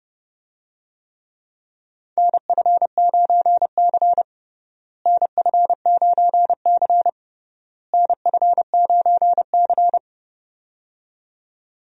Morse code, heard as a keyed tone, about 30 wpm, 700 Hz. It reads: NF9C NF9C NF9C
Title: USA Call Signs - 4 Characters - Repeated 3x - 30wpm